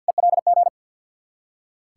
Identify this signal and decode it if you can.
Morse code, heard as a keyed tone, about 46 wpm, 700 Hz. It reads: ELG
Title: Ontario QSO Party Exchange 50wpm